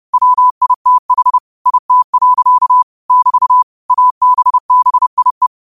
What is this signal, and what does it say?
Morse code, heard as a keyed tone, about 30 wpm, 1000 Hz. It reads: WITH IT. <BT> ABBIE
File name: qotdfiles / 30wpm